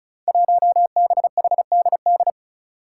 Morse code, heard as a keyed tone, about 34 wpm, 700 Hz. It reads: 1BHDD
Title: Five Challenging Letter-Number Combinations 35wpm